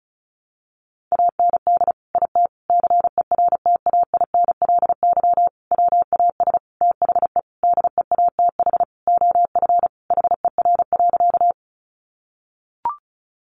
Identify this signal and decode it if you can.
Morse code, heard as a keyed tone, about 34 wpm, 700 Hz. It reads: AND IT CERTAINLY WAS THE DEATH OF HER.
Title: Sentences from Top 1100 Words 35wpm